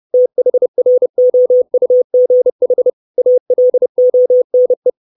Morse code, heard as a keyed tone, about 30 wpm, 500 Hz. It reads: THROUGH ALONE
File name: 632 Words 30WPM